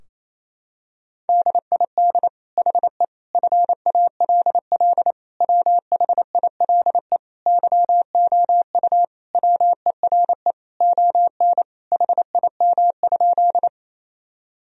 Morse code, heard as a keyed tone, about 28 wpm, 700 Hz. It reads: DID HE FALL WHILE YOU WERE ON HIM?
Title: Sentences from Top 1000 Words - Review Part 2 - 28wpm